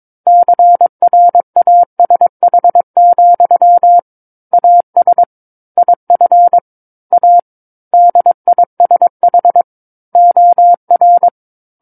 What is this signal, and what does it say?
Morse code, heard as a keyed tone, about 22 wpm, 700 Hz. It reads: CRASH, AS IF A DISH OR
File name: 06-Chapter-06 22 WPM